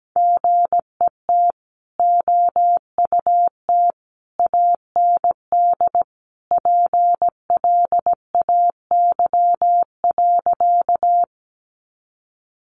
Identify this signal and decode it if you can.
Morse code, heard as a keyed tone, about 17 wpm, 700 Hz. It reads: GET OUT AND PLAY.
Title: Sentences from Top 200 Words 17wpm